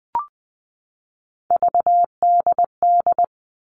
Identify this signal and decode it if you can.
Morse code, heard as a keyed tone, about 20 wpm, 700 Hz. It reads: VDD
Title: All Cut Numbers - 3-Digit 20wpm